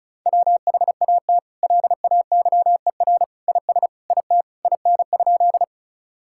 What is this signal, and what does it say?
Morse code, heard as a keyed tone, about 36 wpm, 700 Hz. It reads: WHAT LAYER IS IT IN?
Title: Sentences from Top 1600 Words 35wpm